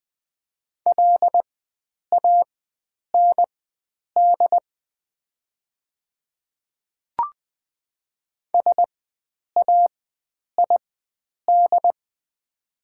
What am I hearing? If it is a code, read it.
Morse code, heard as a keyed tone, about 20 wpm, spaced as for 9 wpm, 700 Hz. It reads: LAND SAID
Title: 4x Character Spacing - Top 300 Words - 20wpm